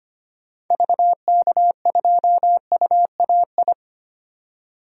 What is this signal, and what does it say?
Morse code, heard as a keyed tone, about 25 wpm, 700 Hz. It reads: VK2UAI
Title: DXCC-100 Call Signs - 6 Characters - 25wpm